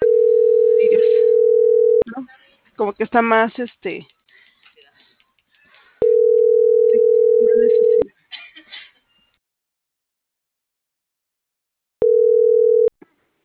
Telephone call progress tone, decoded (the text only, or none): ringback tone